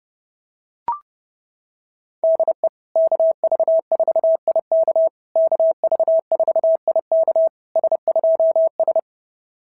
Morse code, heard as a keyed tone, about 30 wpm, 690 Hz. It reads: DE KV4IK KV4IK S2S
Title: SOTA QSO Practice 30wpm